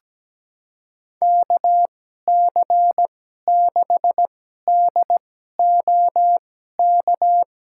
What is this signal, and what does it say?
Morse code, heard as a keyed tone, about 17 wpm, spaced as for 12 wpm, 700 Hz. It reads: KC6DOK
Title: USA Call Signs - 6 Characters - 2x Character Spacing - 17wpm